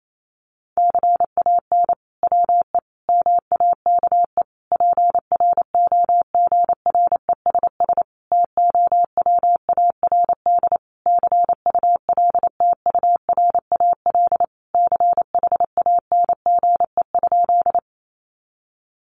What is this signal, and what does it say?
Morse code, heard as a keyed tone, about 28 wpm, 700 Hz. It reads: CAN WE MAKE PROGRESS TOWARD CULTURAL CHANGE?
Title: Sentences from Top 1500 Words 28wpm